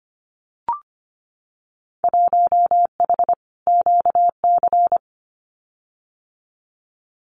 Morse code, heard as a keyed tone, about 25 wpm, 690 Hz. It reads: 1H QC
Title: Winter Field Day Exchanges 25wpm